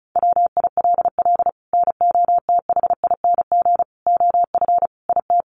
Morse code, heard as a keyed tone, about 34 wpm, 700 Hz. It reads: WILL NOTHING OF IT
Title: Sentences from Top 1200 Words 35wpm